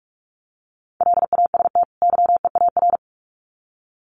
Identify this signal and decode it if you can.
Morse code, heard as a keyed tone, about 44 wpm, 700 Hz. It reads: LAST YEAR
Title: Sets of 2 Words 45wpm